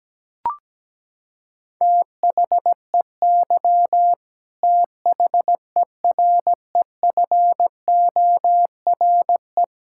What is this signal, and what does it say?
Morse code, heard as a keyed tone, about 17 wpm, 700 Hz. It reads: THEY THEREFORE
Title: Sentences from Top 1000 Words - Review Part 5 - 17wpm